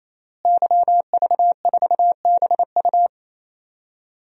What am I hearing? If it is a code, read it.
Morse code, heard as a keyed tone, about 28 wpm, 700 Hz. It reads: YV4BU
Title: DXCC-100 Call Signs - 5 Characters - Repeated 3x - 28wpm